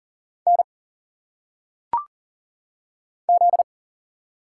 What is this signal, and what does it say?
Morse code, heard as a keyed tone, about 40 wpm, 700 Hz. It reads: N Z
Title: Single Letters - Phonetic Pronunciation 40wpm